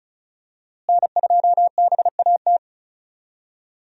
Morse code, heard as a keyed tone, about 36 wpm, 700 Hz. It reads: N2BAT
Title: USA Call Signs - 5 Characters - Repeated 2x - 35wpm